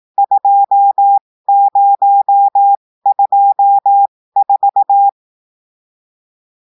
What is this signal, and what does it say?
Morse code, heard as a keyed tone, about 18 wpm, spaced as for 15 wpm, 800 Hz. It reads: 2024
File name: cours-du-lundi files / cours-1580-texte-chiffre-15-mpm